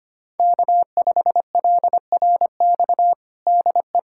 Morse code, heard as a keyed tone, about 25 wpm, 700 Hz. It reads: K5LRX DE
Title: USA Callsign DE Callsign 25wpm